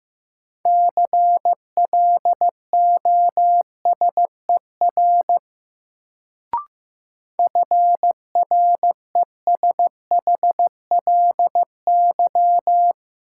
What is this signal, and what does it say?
Morse code, heard as a keyed tone, about 15 wpm, 700 Hz. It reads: CLOSER FRESHLY
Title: Sets of 5 Words Encore - New Word Drill - 15wpm